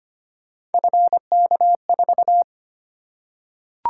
Morse code, heard as a keyed tone, about 25 wpm, 700 Hz. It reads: FK4
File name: Three Letter-Number Combinations - 25wpm